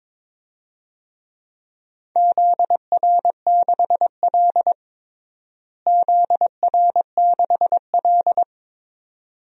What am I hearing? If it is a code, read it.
Morse code, heard as a keyed tone, about 22 wpm, 700 Hz. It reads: ZR6L ZR6L
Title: DXCC-100 Call Signs - 4 Characters - Repeated 2x - 22wpm